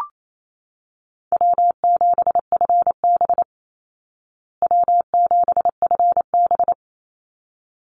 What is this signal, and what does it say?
Morse code, heard as a keyed tone, about 28 wpm, 700 Hz. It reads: W7FB W7FB
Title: Call Signs Repeated 3x - 28wpm